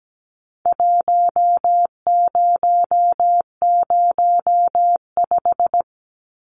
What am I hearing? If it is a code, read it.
Morse code, heard as a keyed tone, about 17 wpm, 700 Hz. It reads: 1005
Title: Numbers - Contest Style - 4-Digit 17wpm